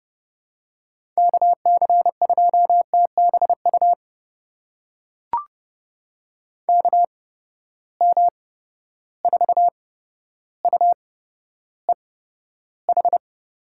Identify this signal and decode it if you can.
Morse code, heard as a keyed tone, about 32 wpm, spaced as for 13 wpm, 700 Hz. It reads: KC2TBU KM4UEH
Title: USA Call Signs - 6 Characters - 8x Character Spacing - 30wpm